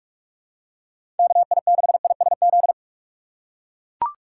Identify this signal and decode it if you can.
Morse code, heard as a keyed tone, about 42 wpm, 700 Hz. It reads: KI6ISZ
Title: USA Call Signs - 3x Character Spacing - 45wpm